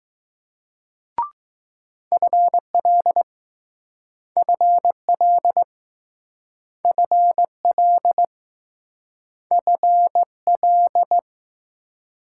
Morse code, heard as a keyed tone, about 20 wpm, 700 Hz. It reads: FL FL FL FL